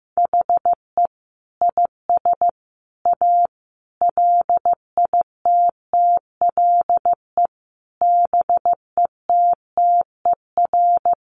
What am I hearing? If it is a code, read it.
Morse code, heard as a keyed tone, about 15 wpm, 700 Hz. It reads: HE IS A LITTLE BETTER
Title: Sentences from Top 600 Words 15wpm